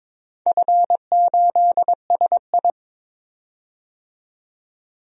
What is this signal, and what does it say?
Morse code, heard as a keyed tone, about 22 wpm, 700 Hz. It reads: F8SI
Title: DXCC-100 Call Signs - 4 Characters - 22wpm